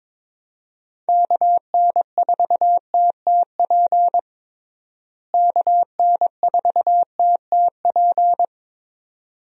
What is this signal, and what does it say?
Morse code, heard as a keyed tone, about 21 wpm, 700 Hz. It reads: KN4TTP KN4TTP
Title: USA Call Signs - 6 Characters - Repeated 2x - 22wpm